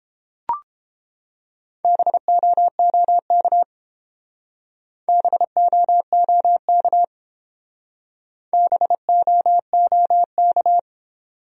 Morse code, heard as a keyed tone, about 30 wpm, 700 Hz. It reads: BOOK BOOK BOOK